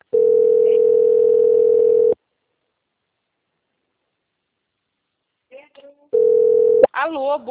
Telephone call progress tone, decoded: ringback tone

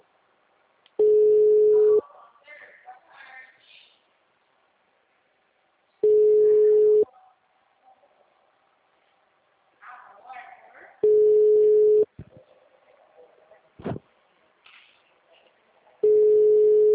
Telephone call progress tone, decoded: ringback tone

